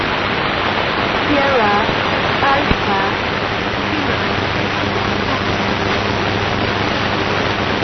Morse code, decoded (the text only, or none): none